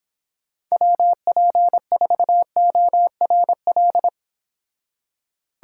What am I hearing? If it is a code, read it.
Morse code, heard as a keyed tone, about 30 wpm, 700 Hz. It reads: WP4ORL